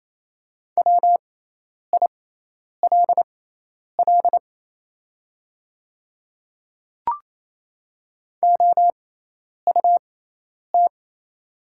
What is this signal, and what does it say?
Morse code, heard as a keyed tone, about 27 wpm, spaced as for 10 wpm, 700 Hz. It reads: WILL OUT
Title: 6x Character Spacing - Top 100 Words - 28wpm